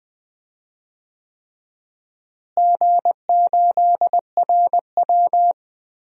Morse code, heard as a keyed tone, about 20 wpm, 700 Hz. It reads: G8RW
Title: DXCC-100 Call Signs - 4 Characters - Repeated 3x - 20wpm